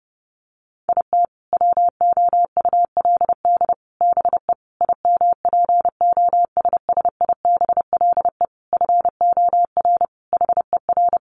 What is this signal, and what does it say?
Morse code, heard as a keyed tone, about 30 wpm, 700 Hz. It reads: IT WOULD BE IMPOSSIBLE FOR HER
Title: Sentences from Top 1100 Words 30wpm